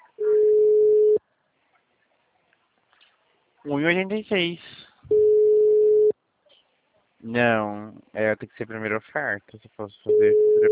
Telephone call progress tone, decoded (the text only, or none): ringback tone